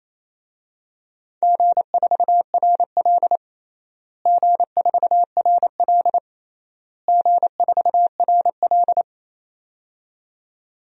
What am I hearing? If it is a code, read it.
Morse code, heard as a keyed tone, about 28 wpm, 700 Hz. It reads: G4RL G4RL G4RL
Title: DXCC-100 Call Signs - 4 Characters - Repeated 3x - 28wpm